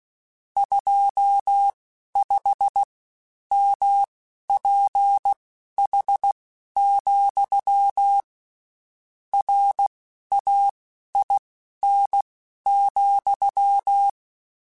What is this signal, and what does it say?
Morse code, heard as a keyed tone, about 16 wpm, spaced as for 11 wpm, 800 Hz. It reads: 25MPH, RAIN,
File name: Spaced Characters / Spaced Characters-mp3s / Weather 5 5 sp @15